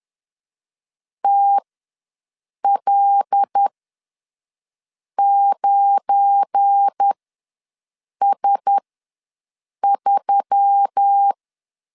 Morse code, heard as a keyed tone, about 10 wpm, spaced as for 5 wpm, 800 Hz. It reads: TL9S3
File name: LEZIONI / 7 GF9Z / 7 GF9 ws 10wpm 4034